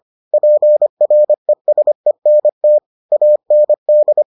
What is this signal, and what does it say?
Morse code, heard as a keyed tone, about 25 wpm, 600 Hz. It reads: PRESENT AND